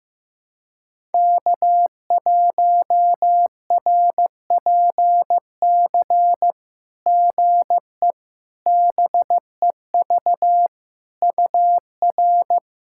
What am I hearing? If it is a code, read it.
Morse code, heard as a keyed tone, about 15 wpm, 700 Hz. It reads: K1RPC GE BEV UR
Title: SOTA QSO Practice 15wpm